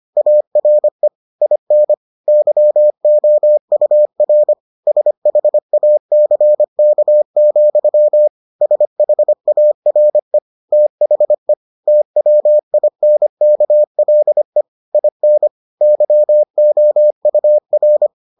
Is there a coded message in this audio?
Yes: Morse code from 0.2 to 18.1 s, heard as a keyed tone, about 25 wpm, 600 Hz. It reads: ARE IN YOUR SHACK, SHARE THE TWINKLE IN YOUR